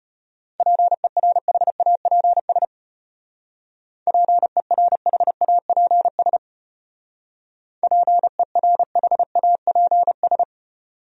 Morse code, heard as a keyed tone, about 33 wpm, 700 Hz. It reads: PERHAPS PERHAPS PERHAPS